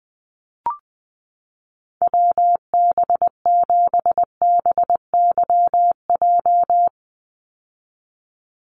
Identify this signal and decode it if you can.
Morse code, heard as a keyed tone, about 20 wpm, 700 Hz. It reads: WB7BYJ
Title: USA Call Signs 20wpm